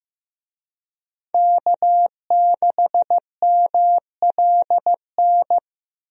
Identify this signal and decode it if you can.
Morse code, heard as a keyed tone, about 15 wpm, 700 Hz. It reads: K6MLN
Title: USA Call Signs - 5 Characters - 15wpm